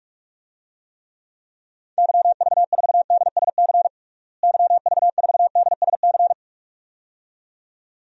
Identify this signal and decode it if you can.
Morse code, heard as a keyed tone, about 46 wpm, 700 Hz. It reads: YV4DSC YV4DSC
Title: DXCC-100 Call Signs - 6 Characters - Repeated 2x - 45wpm